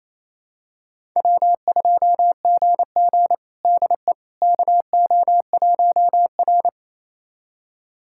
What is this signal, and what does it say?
Morse code, heard as a keyed tone, about 28 wpm, 700 Hz. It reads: W2GG DE KO1R
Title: USA Callsign DE Callsign - 4 Characters - 28wpm